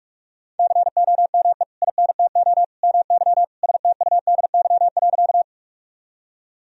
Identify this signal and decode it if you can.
Morse code, heard as a keyed tone, about 42 wpm, 700 Hz. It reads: COME INTO MY STUDY.
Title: Sentences from Top 1000 Words - Review Part 1 - 45wpm